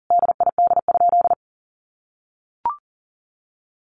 Morse code, heard as a keyed tone, about 40 wpm, 700 Hz. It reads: DID?
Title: Sentences from Top 100 Words 40wpm